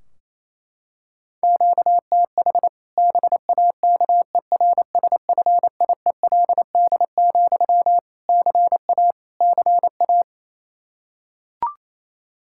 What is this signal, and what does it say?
Morse code, heard as a keyed tone, about 28 wpm, 700 Hz. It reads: QTH BAKERSFIELD, CA CA
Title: Practice QSO - Exchange 1 - Beginning - 28wpm